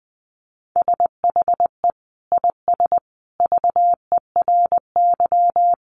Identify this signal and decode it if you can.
Morse code, heard as a keyed tone, about 20 wpm, 700 Hz. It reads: SHE IS VERY
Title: Sentences from Top 1400 Words 20wpm